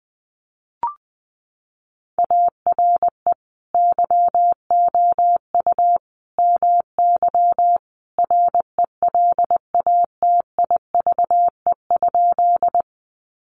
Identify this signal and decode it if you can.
Morse code, heard as a keyed tone, about 20 wpm, 700 Hz. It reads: ARE YOU MY RELATIVE?
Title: Sentences from Top 1400 Words 20wpm